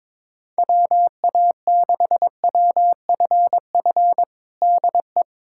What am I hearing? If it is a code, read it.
Morse code, heard as a keyed tone, about 22 wpm, 700 Hz. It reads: WA6WFF DE
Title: USA Callsign DE Callsign 22wpm